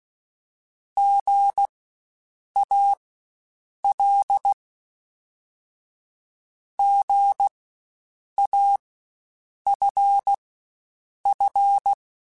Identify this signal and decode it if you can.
Morse code, heard as a keyed tone, about 16 wpm, spaced as for 7 wpm, 800 Hz. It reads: GAL GAFF
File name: Spaced Alphas / Spaced Alphas mp3s / Gs sp @15